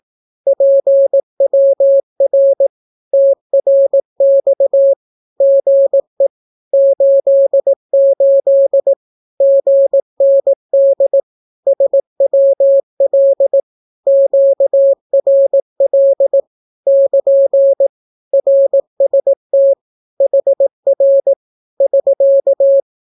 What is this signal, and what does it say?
Morse code, heard as a keyed tone, about 18 wpm, 550 Hz. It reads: PWR TRX GE 88 GND SWL QRL <KN> RST HR <SK>
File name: naukapodstaw / skroty / skroty